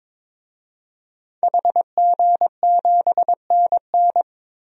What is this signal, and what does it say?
Morse code, heard as a keyed tone, about 22 wpm, 700 Hz. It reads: HG7NN
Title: DXCC-100 Call Signs - 5 Characters - 22wpm